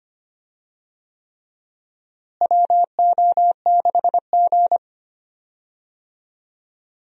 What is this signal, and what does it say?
Morse code, heard as a keyed tone, about 25 wpm, 700 Hz. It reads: WO6G
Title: USA Call Signs - 4 Characters - Repeated 3x - 25wpm